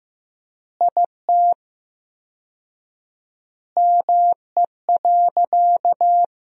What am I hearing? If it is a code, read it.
Morse code, heard as a keyed tone, about 15 wpm, 700 Hz. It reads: IT ME.
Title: 4x Word-Spacing - Sentences from Top 200 Words - 15wpm